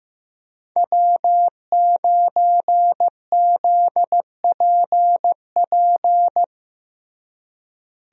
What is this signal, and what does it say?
Morse code, heard as a keyed tone, about 15 wpm, 700 Hz. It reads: W9ZPP
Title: USA Call Signs - 5 Characters - Repeated 3x - 15wpm